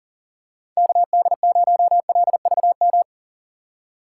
Morse code, heard as a keyed tone, about 40 wpm, 700 Hz. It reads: KD0LVM
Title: USA Call Signs - 6 Characters - 2x Character Spacing - 40wpm